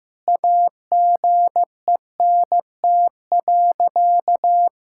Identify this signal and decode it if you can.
Morse code, heard as a keyed tone, about 15 wpm, 700 Hz. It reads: AGENT.
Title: Sentences from Top 1600 Words 15wpm